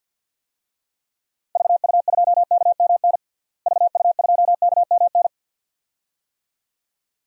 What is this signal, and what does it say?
Morse code, heard as a keyed tone, about 46 wpm, 700 Hz. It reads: VU2KMN VU2KMN
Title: DXCC-100 Call Signs - 6 Characters - Repeated 2x - 50wpm